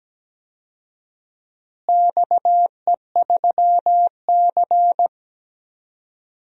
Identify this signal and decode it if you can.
Morse code, heard as a keyed tone, about 17 wpm, 700 Hz. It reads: XE3C
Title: DXCC-100 Call Signs - 4 Characters - Repeated 3x - 17wpm